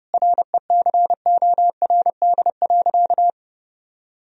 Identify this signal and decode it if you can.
Morse code, heard as a keyed tone, about 30 wpm, 700 Hz. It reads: RECORD.
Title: Sentences from Top 1000 Words - Review Part 3 - 30wpm